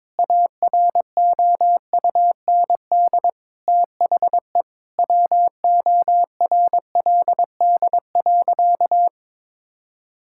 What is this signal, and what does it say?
Morse code, heard as a keyed tone, about 22 wpm, 700 Hz. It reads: AROUND THE WORLD.
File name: practice / 22wpm / Sentences with 8-Word Phrases - 22wpm